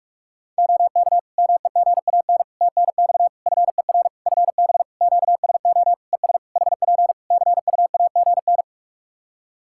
Morse code, heard as a keyed tone, about 44 wpm, 700 Hz. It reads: OK MEGAN TNX FER FB QSO ES HP CUAGN
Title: Practice QSO - Exchange 4 - Ending - 45wpm